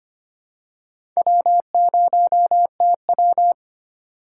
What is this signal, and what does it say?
Morse code, heard as a keyed tone, about 25 wpm, 700 Hz. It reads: W0TW
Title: USA Call Signs - 6x Character Spacing - 25wpm